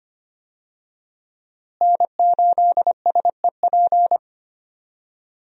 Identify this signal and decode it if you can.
Morse code, heard as a keyed tone, about 25 wpm, 700 Hz. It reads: N8SEP
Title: USA Call Signs - 5 Characters - Repeated 3x - 25wpm